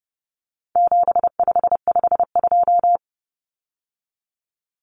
Morse code, heard as a keyed tone, about 30 wpm, 700 Hz. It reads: 7552
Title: Common Cut Numbers - 4-Digit 30wpm